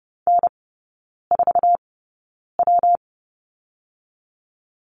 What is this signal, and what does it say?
Morse code, heard as a keyed tone, about 30 wpm, spaced as for 9 wpm, 700 Hz. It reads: N4W
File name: Lesson 19e - Letter w - Three Characters 30wpm